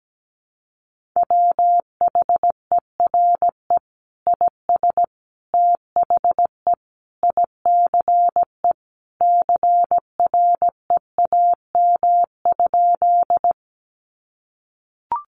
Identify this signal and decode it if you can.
Morse code, heard as a keyed tone, about 17 wpm, 700 Hz. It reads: WHERE IS THE ICE CREAM?
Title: Sentences from Top 1500 Words 17wpm